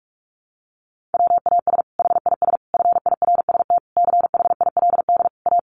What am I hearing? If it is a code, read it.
Morse code, heard as a keyed tone, about 44 wpm, 700 Hz. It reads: WAS HIS FIRST CHILD A
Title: Sentences from Top 800 Words 45wpm